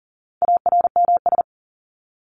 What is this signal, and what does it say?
Morse code, heard as a keyed tone, about 40 wpm, 700 Hz. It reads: ARMS
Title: ES:S Suffix - Top 100 Words with Base Words 40wpm